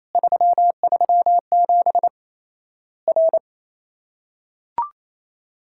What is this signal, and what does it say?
Morse code, heard as a keyed tone, about 28 wpm, 700 Hz. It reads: 337 R
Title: Worked All Europe DX Contest - QTC Exchange - 28wpm